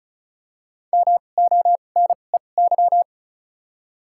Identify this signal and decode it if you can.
Morse code, heard as a keyed tone, about 36 wpm, spaced as for 25 wpm, 700 Hz. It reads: MONEY
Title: 2x Character Spacing - Top 500 Words - 35wpm